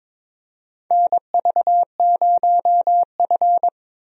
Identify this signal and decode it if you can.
Morse code, heard as a keyed tone, about 22 wpm, 700 Hz. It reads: NV0F
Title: USA Call Signs - 4 Characters - Repeated 3x - 22wpm